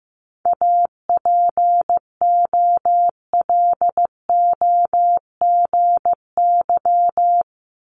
Morse code, heard as a keyed tone, about 15 wpm, 700 Hz. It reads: APOLOGY
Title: Multi-letter Phonemes Repeated 3x 15wpm